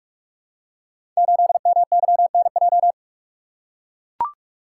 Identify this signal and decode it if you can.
Morse code, heard as a keyed tone, about 44 wpm, 700 Hz. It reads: 8MYNJ
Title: Code Groups - 5 Letter:Numbers - 45wpm